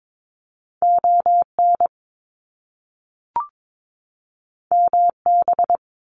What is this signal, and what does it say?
Morse code, heard as a keyed tone, about 22 wpm, 700 Hz. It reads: ON MB
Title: Canadian Provinces and Territories 22wpm